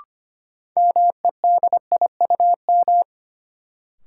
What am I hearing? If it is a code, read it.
Morse code, heard as a keyed tone, about 25 wpm, 700 Hz. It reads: MEDIUM
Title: Sentences from Top 900 Words - New Word Drill - 25wpm